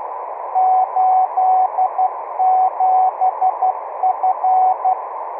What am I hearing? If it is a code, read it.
Morse code, heard as a keyed tone, about 12 wpm, 750 Hz, with noise. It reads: 87F
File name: k7tmg / Temp2Morse